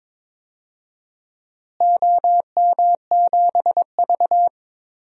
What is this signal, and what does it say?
Morse code, heard as a keyed tone, about 22 wpm, 700 Hz. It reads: OM7V
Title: DXCC-100 Call Signs - 4 Characters - Repeated 2x - 22wpm